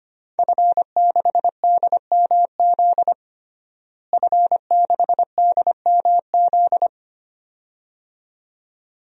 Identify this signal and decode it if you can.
Morse code, heard as a keyed tone, about 25 wpm, 700 Hz. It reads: F6DMZ F6DMZ
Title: DXCC-100 Call Signs - 5 Characters - Repeated 2x - 25wpm